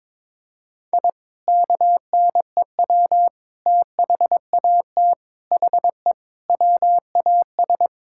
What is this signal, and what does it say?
Morse code, heard as a keyed tone, about 22 wpm, 700 Hz. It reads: I KNEW THAT HE WAS
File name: practice / 22wpm / Sentences with 3-Word Phrases - 22wpm